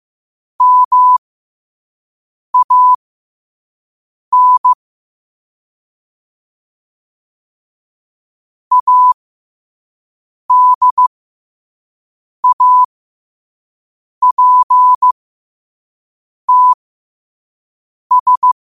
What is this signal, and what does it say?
Morse code, heard as a keyed tone, about 15 wpm, spaced as for 5 wpm, 1000 Hz. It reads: MAN ADAPTS
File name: qotdfiles / 5wpm